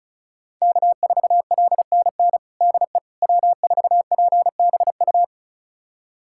Morse code, heard as a keyed tone, about 36 wpm, 700 Hz. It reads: K4LNN DE W4PBU
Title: USA Callsign DE Callsign - 5 Characters - 35wpm